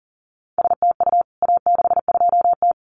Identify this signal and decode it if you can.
Morse code, heard as a keyed tone, about 40 wpm, 700 Hz. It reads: STU A62T
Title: CWT Exchange 40wpm